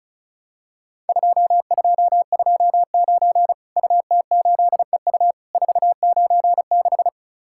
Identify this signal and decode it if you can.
Morse code, heard as a keyed tone, about 36 wpm, 700 Hz. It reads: 2229 UT8EU 496
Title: Worked All Europe DX Contest - QTC Exchange - 35wpm